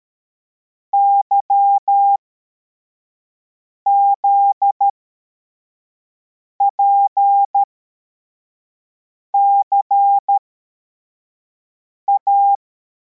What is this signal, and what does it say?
Morse code, heard as a keyed tone, about 13 wpm, spaced as for 4 wpm, 800 Hz. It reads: YZPCA
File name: formation / cw / Text73